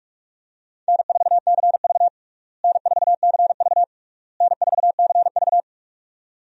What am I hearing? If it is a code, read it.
Morse code, heard as a keyed tone, about 46 wpm, 700 Hz. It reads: N4CV N4CV N4CV
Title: USA Call Signs - 4 Characters - Repeated 3x - 45wpm